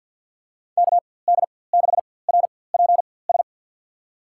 Morse code, heard as a keyed tone, about 52 wpm, spaced as for 23 wpm, 700 Hz. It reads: KD6RPS
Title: USA Call Signs - 6 Characters - 4x Character Spacing - 50wpm